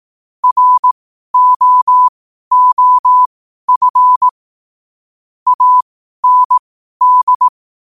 Morse code, heard as a keyed tone, about 18 wpm, spaced as for 12 wpm, 1000 Hz. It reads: ROOF AND
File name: qotdfiles / 12wpm